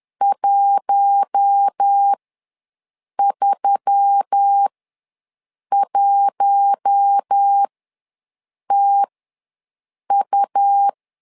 Morse code, heard as a keyed tone, about 10 wpm, spaced as for 6 wpm, 800 Hz. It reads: 131TU
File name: LEZIONI / 4 Q1S3 ws 10wpm 2437